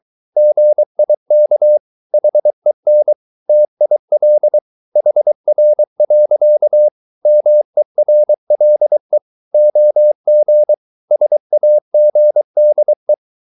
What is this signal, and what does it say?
Morse code, heard as a keyed tone, about 23 wpm, 600 Hz. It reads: GIK HEN TIL HR. MERLE OG SAGDE
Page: Ferie 23 wpm